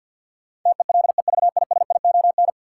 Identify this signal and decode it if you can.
Morse code, heard as a keyed tone, about 50 wpm, 700 Hz. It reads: TELEVISION
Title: Sentences from Top 1000 Words - Review Part 4 - New Word Drill - 50wpm